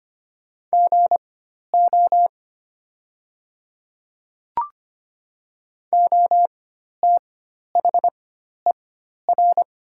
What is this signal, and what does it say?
Morse code, heard as a keyed tone, about 25 wpm, spaced as for 12 wpm, 700 Hz. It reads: GO OTHER
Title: 4x Character Spacing - Top 100 Words - 25wpm